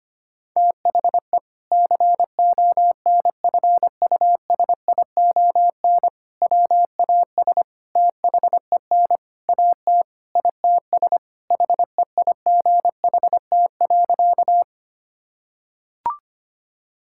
Morse code, heard as a keyed tone, about 25 wpm, 700 Hz. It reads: THE CONFUSION WAS THEN AT ITS HEIGHT.
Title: Sentences from Top 1600 Words 25wpm